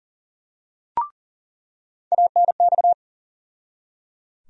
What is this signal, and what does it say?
Morse code, heard as a keyed tone, about 37 wpm, 700 Hz. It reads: ANX
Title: USA Call Sign Postfixes - 3 Letters 40wpm